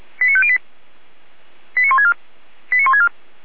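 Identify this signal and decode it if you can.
Five-tone selective call (ZVEI), heard as quick runs of five tones, tones of about 70 ms. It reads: 89508 89164 89164
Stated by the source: ZVEI five-tone selective call received via SDR.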